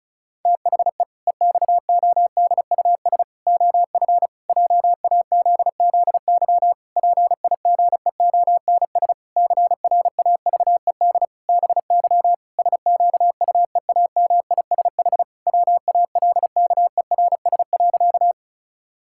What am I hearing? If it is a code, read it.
Morse code, heard as a keyed tone, about 34 wpm, 700 Hz. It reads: THE EXODUS OF JAZZY PIGEONS CRAVED BY SQUEAMISH WALKERS.
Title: Pangram Challenge v2 - 35wpm